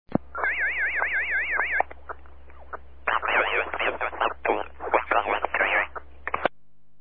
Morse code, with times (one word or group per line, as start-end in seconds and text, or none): none